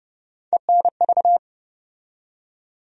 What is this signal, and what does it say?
Morse code, heard as a keyed tone, about 30 wpm, 700 Hz. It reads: ENV
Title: USA Call Sign Postfixes - 3 Letters 30wpm